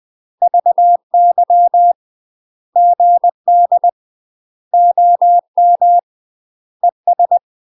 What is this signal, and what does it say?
Morse code, heard as a keyed tone, about 20 wpm, 700 Hz. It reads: VY GD OM ES
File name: UK MASTER QSOs TRAINING / 20WPM / UK MASTER QSOs 0049